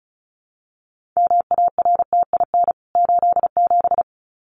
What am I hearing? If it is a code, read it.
Morse code, heard as a keyed tone, about 34 wpm, 700 Hz. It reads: MARTIN 87
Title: CWT Exchange 35wpm